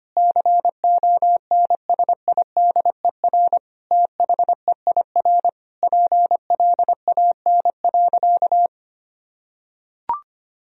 Morse code, heard as a keyed tone, about 25 wpm, 690 Hz. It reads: CONSIDER THEIR PLAN.
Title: Sentences from Top 1000 Words - Review Part 4 - 25wpm